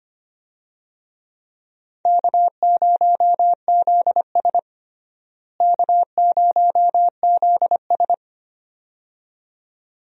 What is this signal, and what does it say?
Morse code, heard as a keyed tone, about 25 wpm, 700 Hz. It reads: K0ZS K0ZS
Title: USA Call Signs - 4 Characters - Repeated 2x - 25wpm